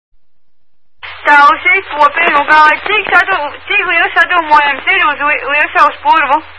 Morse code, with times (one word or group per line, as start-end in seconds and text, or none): none